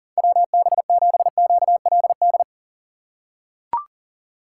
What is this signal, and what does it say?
Morse code, heard as a keyed tone, about 39 wpm, 700 Hz. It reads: WB7QLD
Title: USA Call Signs - 6 Characters - 3x Character Spacing - 40wpm